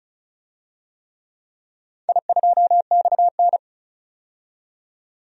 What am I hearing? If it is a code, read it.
Morse code, heard as a keyed tone, about 36 wpm, 700 Hz. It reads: I2XN
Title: DXCC-100 Call Signs - 4 Characters - 35wpm